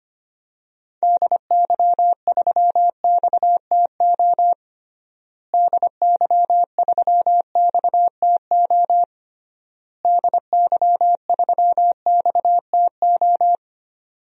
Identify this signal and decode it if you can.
Morse code, heard as a keyed tone, about 25 wpm, 700 Hz. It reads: DY3XTO DY3XTO DY3XTO
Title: DXCC-100 Call Signs - 6 Characters - Repeated 3x - 25wpm